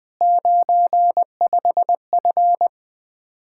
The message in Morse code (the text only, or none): 95F